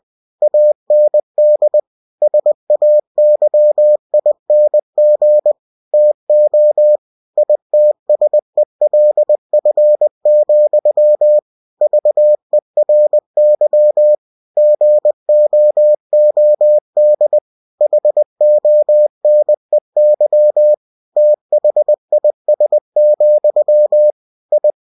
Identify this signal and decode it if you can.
Morse code, heard as a keyed tone, about 20 wpm, 600 Hz. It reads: AND SAYING TO ITSELF, VERY GOOD HONEY THIS, I
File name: cw-books / winnie-the-pooh / winnie-20wpm-page-0040